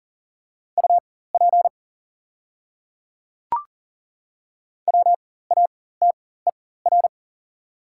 Morse code, heard as a keyed tone, about 40 wpm, spaced as for 19 wpm, 690 Hz. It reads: UP WATER
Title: 4x Character Spacing - Top 300 Words - 40wpm